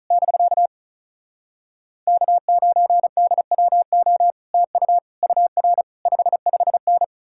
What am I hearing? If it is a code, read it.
Morse code, heard as a keyed tone, about 46 wpm, spaced as for 41 wpm, 700 Hz. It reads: <BK> K9DWO TU UR 55N
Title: Speed-Racing Common QSO Phrases - 35wpm